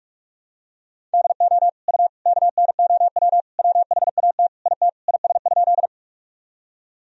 Morse code, heard as a keyed tone, about 46 wpm, 700 Hz. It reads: DO U KNOW WHAT IT IS?
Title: Sentences from Top 100 Words and Abbreviations 45wpm